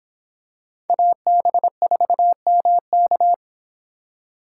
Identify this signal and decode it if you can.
Morse code, heard as a keyed tone, about 26 wpm, 700 Hz. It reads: AB4MK